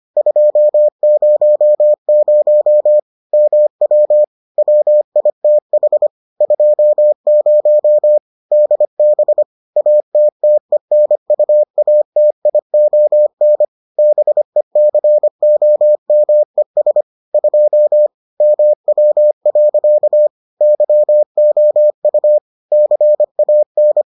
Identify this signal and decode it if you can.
Morse code, heard as a keyed tone, about 25 wpm, 600 Hz. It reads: 200 MW WITH 20 DB ATTENUATION BECOMES 2 MW. YOU CAN